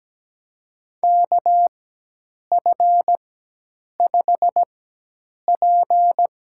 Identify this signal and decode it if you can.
Morse code, heard as a keyed tone, about 17 wpm, spaced as for 8 wpm, 700 Hz. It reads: KF5P
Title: USA Call Signs - 4x Character Spacing - 17wpm